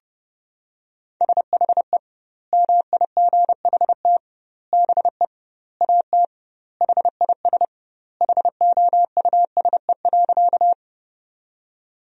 Morse code, heard as a keyed tone, about 32 wpm, 700 Hz. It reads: SHE MIGHT BE AT HIS HOUSE.
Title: 2x Word-Spacing - Sentences from Top 300 Words - 30wpm